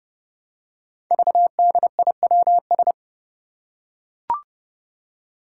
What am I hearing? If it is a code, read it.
Morse code, heard as a keyed tone, about 30 wpm, 700 Hz. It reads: VDIWS
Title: Five Challenging Letter-Number Combinations 30wpm